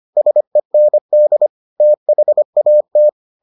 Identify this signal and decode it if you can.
Morse code, heard as a keyed tone, about 25 wpm, 600 Hz. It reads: SEND THAT